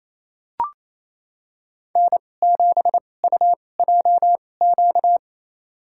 Morse code, heard as a keyed tone, about 28 wpm, spaced as for 20 wpm, 700 Hz. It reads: N7UJQ
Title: USA Call Signs - 2x Character Spacing - 28wpm